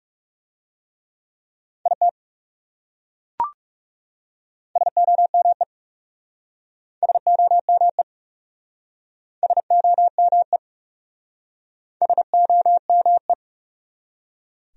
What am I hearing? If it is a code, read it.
Morse code, heard as a keyed tone, about 40 wpm, 700 Hz. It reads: IT SOME SOME SOME SOME